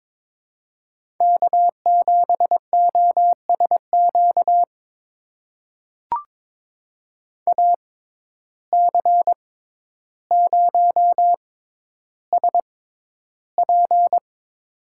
Morse code, heard as a keyed tone, about 22 wpm, spaced as for 11 wpm, 700 Hz. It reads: K7OSQ AC0SP
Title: USA Call Signs - 5 Characters - 6x Character Spacing - 22wpm